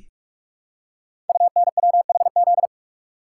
Morse code, heard as a keyed tone, about 44 wpm, 690 Hz. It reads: UNWHZ
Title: Code Groups - 5 Letters - 45wpm